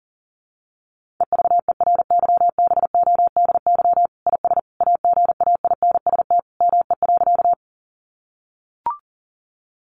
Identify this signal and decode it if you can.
Morse code, heard as a keyed tone, about 40 wpm, 700 Hz. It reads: EVERYBODY IS AGAINST ME.
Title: Sentences from Top 1400 Words 40wpm